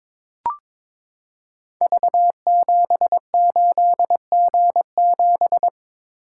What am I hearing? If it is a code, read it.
Morse code, heard as a keyed tone, about 22 wpm, 700 Hz. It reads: V78G7
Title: Code Groups - 5 Letter:Numbers - 22wpm